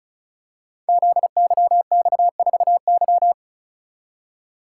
Morse code, heard as a keyed tone, about 36 wpm, 690 Hz. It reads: ZYX4Y
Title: Code Groups - 5 Letter:Numbers - 35wpm